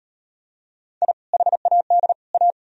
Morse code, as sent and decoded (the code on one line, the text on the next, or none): .. / .... .- -.. / .-
I HAD A